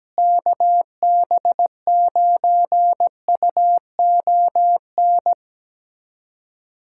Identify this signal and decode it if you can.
Morse code, heard as a keyed tone, about 17 wpm, 700 Hz. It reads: KB9UON
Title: USA Callsign DE Callsign - 6 Characters - 17wpm